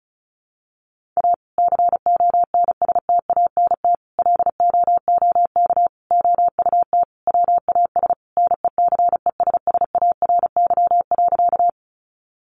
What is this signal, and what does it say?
Morse code, heard as a keyed tone, about 34 wpm, 700 Hz. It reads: A CONSTANT LOOK OUT WAS NECESSARY.
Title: Sentences from Top 1500 Words 35wpm